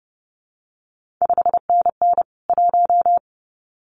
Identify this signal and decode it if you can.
Morse code, heard as a keyed tone, about 30 wpm, 700 Hz. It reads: 5NN 1
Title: CQ WW DX Contest Exchange Encore - 30wpm